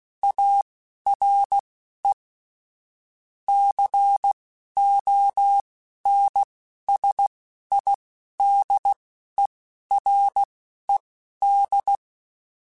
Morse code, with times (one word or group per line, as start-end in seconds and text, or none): 0.2-2.1 ARE
3.5-12.0 CONSIDERED